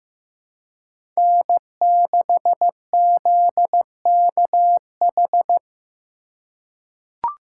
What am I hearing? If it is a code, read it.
Morse code, heard as a keyed tone, about 15 wpm, 700 Hz. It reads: N6ZKH
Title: USA Call Signs - 5 Characters - 4x Character Spacing - 15wpm